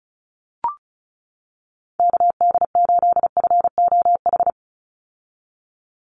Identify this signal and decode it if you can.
Morse code, heard as a keyed tone, about 34 wpm, 700 Hz. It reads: KD8FOH
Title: USA Call Signs 35wpm